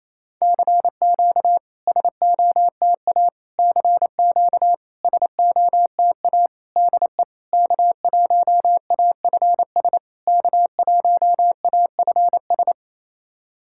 Morse code, heard as a keyed tone, about 28 wpm, 690 Hz. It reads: CQ SOTA CQ SOTA DE K1AFS K1AFS
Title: SOTA QSO Practice 28wpm